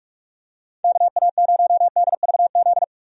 Morse code, heard as a keyed tone, about 46 wpm, 700 Hz. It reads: KA0DVZ
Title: USA Call Signs - 6 Characters - 45wpm